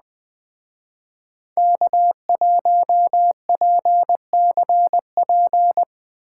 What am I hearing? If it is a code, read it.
Morse code, heard as a keyed tone, about 20 wpm, 700 Hz. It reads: K1PCP
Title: Speed Racing - USA Call Signs 17wpm